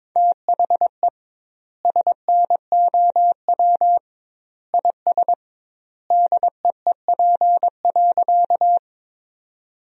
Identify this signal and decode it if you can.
Morse code, heard as a keyed tone, about 22 wpm, 700 Hz. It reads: THE SNOW IS DEEP.
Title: 2x Word-Spacing - Sentences from Top 500 Words - 22wpm